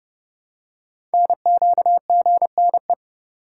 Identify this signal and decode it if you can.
Morse code, heard as a keyed tone, about 30 wpm, 700 Hz. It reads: NQGNE
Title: Code Groups - 5 Letters - 30wpm